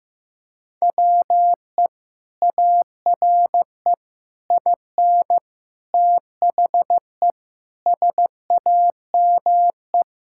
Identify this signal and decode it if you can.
Morse code, heard as a keyed tone, about 15 wpm, 700 Hz. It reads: WE ARE IN THE SAME
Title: Sentences from Top 1000 Words - Review Part 6 - 15wpm